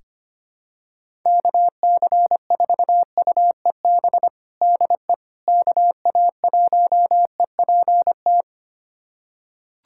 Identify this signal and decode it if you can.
Morse code, heard as a keyed tone, about 25 wpm, 700 Hz. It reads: KC4UEB DE KA1EPT
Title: USA Callsign DE Callsign - 6 Characters - 25wpm